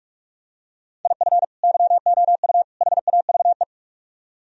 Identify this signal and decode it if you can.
Morse code, heard as a keyed tone, about 45 wpm, 700 Hz. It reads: IF YOU HAVE